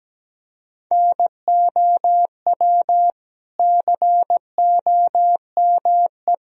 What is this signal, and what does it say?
Morse code, heard as a keyed tone, about 17 wpm, 700 Hz. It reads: NOW COME
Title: Sentences from Top 200 Words and Abbreviations 17wpm